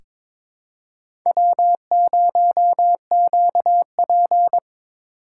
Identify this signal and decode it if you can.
Morse code, heard as a keyed tone, about 22 wpm, 700 Hz. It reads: W0QP
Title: USA Call Signs - 4 Characters - 2x Character Spacing - 22wpm